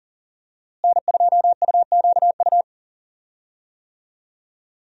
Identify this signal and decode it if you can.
Morse code, heard as a keyed tone, about 40 wpm, 700 Hz. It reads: N2UQU
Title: USA Call Signs - 5 Characters - Repeated 2x - 40wpm